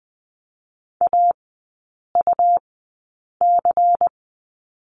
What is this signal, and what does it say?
Morse code, heard as a keyed tone, about 20 wpm, spaced as for 8 wpm, 700 Hz. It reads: AUC
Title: 127 Lesson 21e - Number 6 - Three Characters 20wpm